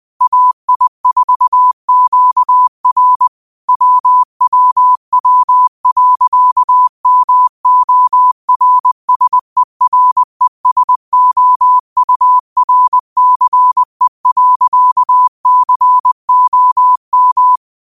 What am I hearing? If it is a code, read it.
Morse code, heard as a keyed tone, about 20 wpm, 1000 Hz. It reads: AI4QR WWW.MORSERESOURCE.COM